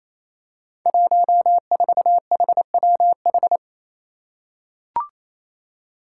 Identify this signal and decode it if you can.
Morse code, heard as a keyed tone, about 28 wpm, 700 Hz. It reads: 14HWH
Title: Five Challenging Letter-Number Combinations 28wpm